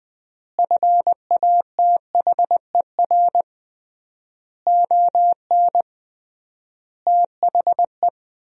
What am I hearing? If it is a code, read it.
Morse code, heard as a keyed tone, about 20 wpm, 700 Hz. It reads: FATHER ON THE
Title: 3x Word-Spacing - Sentences from Top 500 Words - 20wpm